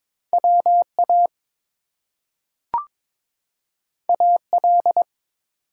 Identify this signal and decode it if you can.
Morse code, heard as a keyed tone, about 22 wpm, 700 Hz. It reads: WA AL
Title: US States and Canadian Provinces 22wpm